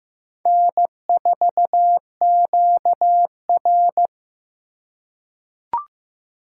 Morse code, heard as a keyed tone, about 15 wpm, 700 Hz. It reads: N4QR
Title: USA Call Signs - 4 Characters - 2x Character Spacing - 15wpm